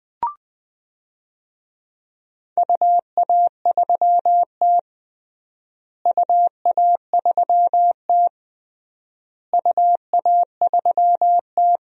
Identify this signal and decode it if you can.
Morse code, heard as a keyed tone, about 20 wpm, 700 Hz. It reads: UA3T UA3T UA3T
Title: DXCC-100 Call Signs - 4 Characters - Repeated 3x - 20wpm